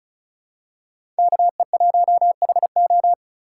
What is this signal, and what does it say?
Morse code, heard as a keyed tone, about 36 wpm, 700 Hz. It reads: KE1HO
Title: USA Call Signs - 5 Characters - 3x Character Spacing - 35wpm